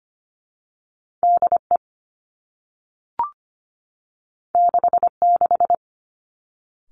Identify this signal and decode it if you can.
Morse code, heard as a keyed tone, about 25 wpm, 700 Hz. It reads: D E 6 6
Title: Two Challenging Letter-Number Combinations 25wpm